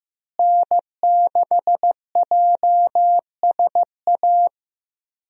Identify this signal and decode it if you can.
Morse code, heard as a keyed tone, about 16 wpm, 700 Hz. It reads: N6JSA
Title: USA Call Signs - 5 Characters - Repeated 3x - 15wpm